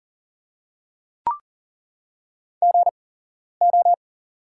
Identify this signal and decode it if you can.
Morse code, heard as a keyed tone, about 40 wpm, spaced as for 11 wpm, 690 Hz. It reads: GO
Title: 8x Character Spacing - Top 300 Words - 40wpm